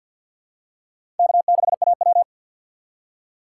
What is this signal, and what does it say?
Morse code, heard as a keyed tone, about 52 wpm, 700 Hz. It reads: K6AW
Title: USA Call Signs - 6x Character Spacing - 50wpm